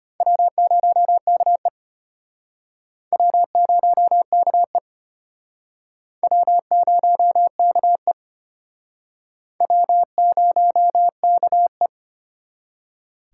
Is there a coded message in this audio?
Yes: Morse code from 0.2 to 11.9 s, heard as a keyed tone, about 33 wpm, 700 Hz. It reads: W0KE W0KE W0KE W0KE